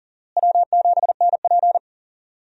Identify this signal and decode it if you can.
Morse code, heard as a keyed tone, about 40 wpm, 700 Hz. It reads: W7NP
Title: USA Call Signs - 4 Characters - Repeated 3x - 40wpm